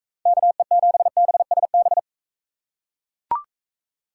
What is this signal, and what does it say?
Morse code, heard as a keyed tone, about 40 wpm, 700 Hz. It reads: KE7BSB